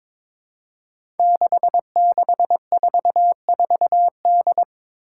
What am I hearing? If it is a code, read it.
Morse code, heard as a keyed tone, about 22 wpm, 700 Hz. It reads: 6644D
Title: Five Challenging Letter-Number Combinations 22wpm